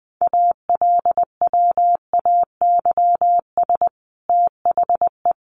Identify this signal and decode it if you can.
Morse code, heard as a keyed tone, about 20 wpm, 700 Hz. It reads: ALWAYS THE
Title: Sentences from Top 300 Words 20wpm